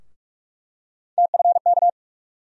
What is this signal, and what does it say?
Morse code, heard as a keyed tone, about 46 wpm, 700 Hz. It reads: TFK
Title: USA Call Sign Postfixes - 3 Letters 45wpm